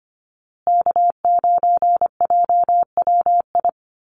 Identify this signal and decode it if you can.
Morse code, heard as a keyed tone, about 25 wpm, 700 Hz. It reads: K9JWI
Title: Call Signs Repeated 2x - 25wpm